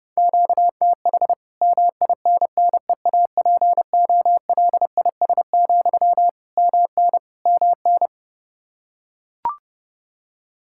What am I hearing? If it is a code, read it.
Morse code, heard as a keyed tone, about 30 wpm, 700 Hz. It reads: QTH MINNEAPOLIS, MN MN
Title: Practice QSO - Exchange 1 - Beginning - 30wpm